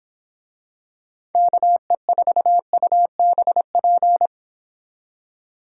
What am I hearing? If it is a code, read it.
Morse code, heard as a keyed tone, about 26 wpm, 700 Hz. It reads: KE4UBP
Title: Speed Racing - USA 2x3 Call Signs 22wpm